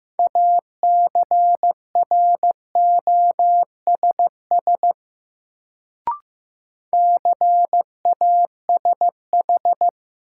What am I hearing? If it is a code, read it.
Morse code, heard as a keyed tone, about 15 wpm, 700 Hz. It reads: ACROSS CASH
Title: Sentences from Top 700 Words - New Word Drill - 15wpm